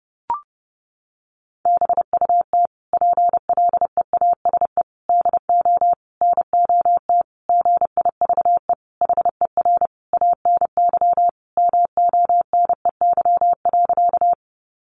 Morse code, heard as a keyed tone, about 30 wpm, 700 Hz. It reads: BUT PLEASE DO NOT GIVE HER ANY MONEY.
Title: Sentences from Top 1500 Words 30wpm